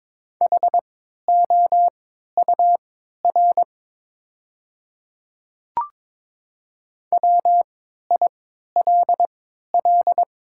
Morse code, heard as a keyed tone, about 22 wpm, spaced as for 13 wpm, 700 Hz. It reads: HOUR WILL
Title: 3x Character Spacing - Top 500 Words - 22wpm